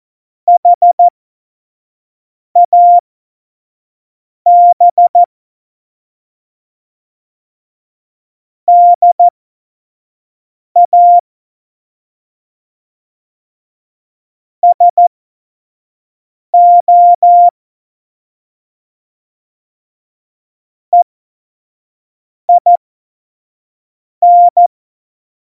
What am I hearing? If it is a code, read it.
Morse code, heard as a keyed tone, about 14 wpm, spaced as for 5 wpm, 700 Hz. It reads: HAB DA SO EIN